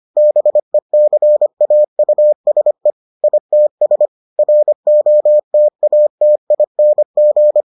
Morse code, heard as a keyed tone, about 25 wpm, 600 Hz. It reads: BECAUSE ITS ROTATING